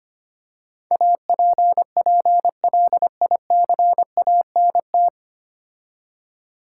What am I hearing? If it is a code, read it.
Morse code, heard as a keyed tone, about 25 wpm, 700 Hz. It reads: APPLICANT
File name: practice / 25wpm / 9 Letter Words - First 3 Letters Given - 25wpm